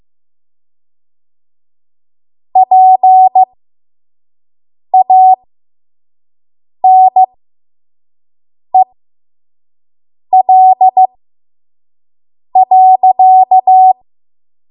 Morse code, heard as a keyed tone, about 15 wpm, spaced as for 5 wpm, 750 Hz. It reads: PANEL.
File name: Morse / 05WPM